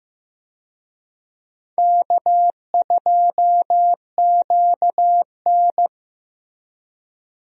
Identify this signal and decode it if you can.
Morse code, heard as a keyed tone, about 16 wpm, 700 Hz. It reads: K2QN
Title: USA Call Signs - 4 Characters - Repeated 2x - 15wpm